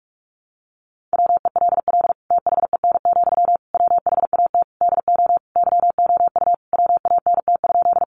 Morse code, heard as a keyed tone, about 44 wpm, 700 Hz. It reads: WELL THEN, WHAT DO YOU WANT?
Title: Sentences from Top 200 Words 45wpm